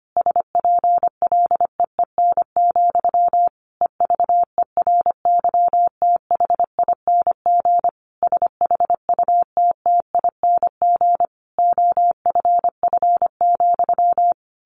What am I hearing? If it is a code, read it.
Morse code, heard as a keyed tone, about 25 wpm, 700 Hz. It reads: SPLEEN, EVERYTHING SHUTTING OFF,
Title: A Sound Of Thunder 25wpm